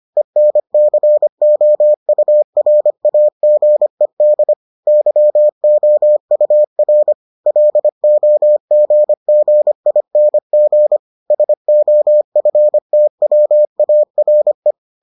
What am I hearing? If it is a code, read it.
Morse code, heard as a keyed tone, about 25 wpm, 600 Hz. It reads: ENCOURAGED YOUR LOGGING SOFTWARE